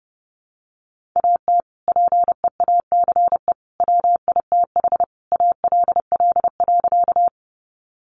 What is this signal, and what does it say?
Morse code, heard as a keyed tone, about 30 wpm, 700 Hz. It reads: AT PEACE WITH ALL.
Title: Sentences from Top 1500 Words 30wpm